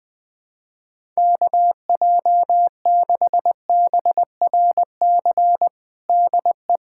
Morse code, heard as a keyed tone, about 20 wpm, 700 Hz. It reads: KJ6BRC DE
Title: USA Callsign DE Callsign - 6 Characters - 20wpm